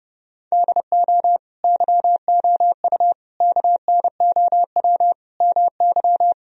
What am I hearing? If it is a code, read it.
Morse code, heard as a keyed tone, about 30 wpm, 700 Hz. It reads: DO YOU KNOW MY
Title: Sentences from Top 1000 Words - Review Part 3 - 30wpm